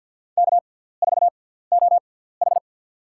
Morse code, heard as a keyed tone, about 47 wpm, spaced as for 17 wpm, 700 Hz. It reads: K4OH
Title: USA Call Signs - 4 Characters - 6x Character Spacing - 50wpm